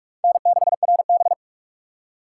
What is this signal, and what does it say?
Morse code, heard as a keyed tone, about 40 wpm, 700 Hz. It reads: N6RB